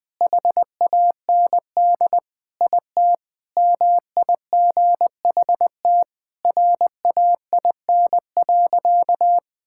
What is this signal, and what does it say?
Morse code, heard as a keyed tone, about 20 wpm, 700 Hz. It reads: HAND IT MIGHT RAIN.
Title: Sentences with 3-Word Phrases - 20wpm